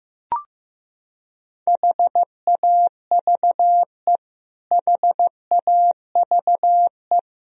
Morse code, heard as a keyed tone, about 16 wpm, 700 Hz. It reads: HAVE HAVE
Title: Top 100 Words Repeated 15wpm